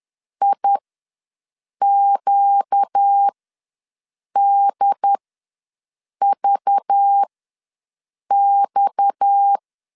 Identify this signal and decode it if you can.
Morse code, heard as a keyed tone, about 10 wpm, spaced as for 6 wpm, 800 Hz. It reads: IQDVX
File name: LEZIONI / 6 PKX0 / 6 PKX0 ws 10wpm 4220